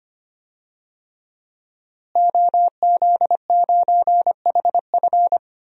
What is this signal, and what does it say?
Morse code, heard as a keyed tone, about 25 wpm, 700 Hz. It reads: OZ9HF
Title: DXCC-100 Call Signs - 5 Characters - Repeated 3x - 25wpm